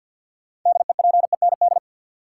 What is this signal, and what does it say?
Morse code, heard as a keyed tone, about 52 wpm, 700 Hz. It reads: DEPEND
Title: Sentences from Top 1300 Words - New Word Drill Alt - 50wpm